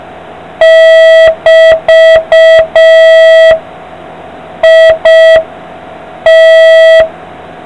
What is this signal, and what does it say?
Morse code, heard as a keyed tone, about 5 wpm, 650 Hz. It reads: <BT>IT